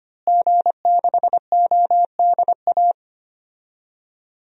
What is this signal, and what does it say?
Morse code, heard as a keyed tone, about 25 wpm, 700 Hz. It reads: G6ODA
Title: DXCC-100 Call Signs - 5 Characters - Repeated 2x - 25wpm